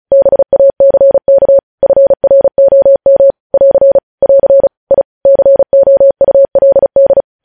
Morse code, heard as a keyed tone, about 36 wpm, 550 Hz. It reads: BACK FROM <AR> <AR> I COULD
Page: news headlines in morse code at 35 wpm